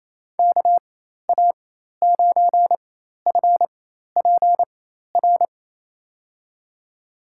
Morse code, heard as a keyed tone, about 28 wpm, spaced as for 13 wpm, 700 Hz. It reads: KA9FPR
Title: USA Call Signs - 6 Characters - 4x Character Spacing - 28wpm